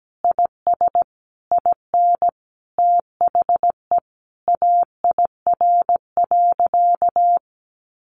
Morse code, heard as a keyed tone, about 17 wpm, 700 Hz. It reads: IS IN THE AIR.
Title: Sentences from Top 600 Words 17wpm